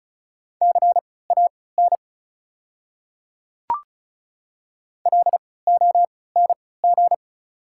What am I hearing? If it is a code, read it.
Morse code, heard as a keyed tone, about 36 wpm, spaced as for 20 wpm, 700 Hz. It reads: CAN LONG
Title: 3x Character Spacing - Top 200 Words - 35wpm